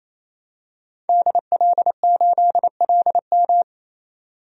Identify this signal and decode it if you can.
Morse code, heard as a keyed tone, about 28 wpm, 700 Hz. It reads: DL8LM
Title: DXCC-100 Call Signs - 5 Characters - 28wpm